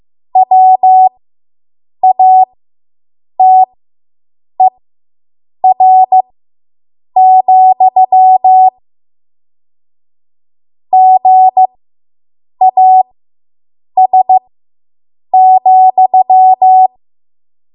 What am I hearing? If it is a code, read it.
Morse code, heard as a keyed tone, about 15 wpm, spaced as for 7 wpm, 750 Hz. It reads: WATER, GAS,